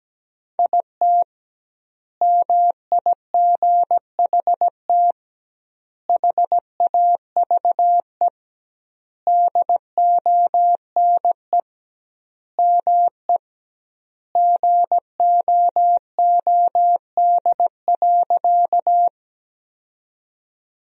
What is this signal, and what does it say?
Morse code, heard as a keyed tone, about 17 wpm, 700 Hz. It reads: IT MIGHT HAVE DONE ME GOOD.
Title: 2x Word-Spacing - Sentences from Top 500 Words - 17wpm